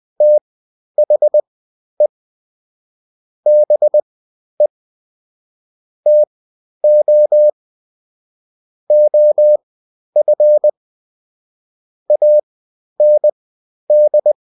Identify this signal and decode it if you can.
Morse code, heard as a keyed tone, about 20 wpm, spaced as for 11 wpm, 600 Hz. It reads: THE BE TO OF AND
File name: Top-100-Words-10